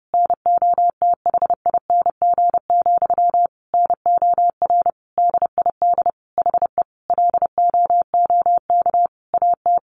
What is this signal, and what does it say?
Morse code, heard as a keyed tone, about 30 wpm, 700 Hz. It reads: NOTHING, NOR DID HE LOOK AT
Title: Sentences from Top 1500 Words 30wpm